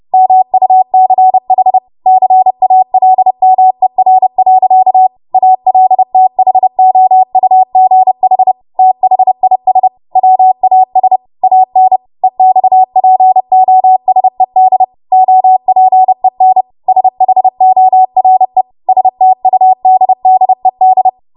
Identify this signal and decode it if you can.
Morse code, heard as a keyed tone, about 30 wpm, 750 Hz. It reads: MUCH CALMER. ALTHOUGH THIS WAS AN EXPOSED OPEN SHORE STUDDED
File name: Morse / 30WPM